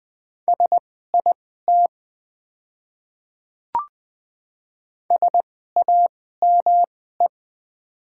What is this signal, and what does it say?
Morse code, heard as a keyed tone, about 21 wpm, spaced as for 15 wpm, 690 Hz. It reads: SIT SAME
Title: 2x Character Spacing - Top 500 Words - 20wpm